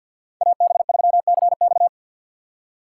Morse code, heard as a keyed tone, about 52 wpm, 700 Hz. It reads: AB3CX
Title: USA Call Signs - 5 Characters - 4x Character Spacing - 50wpm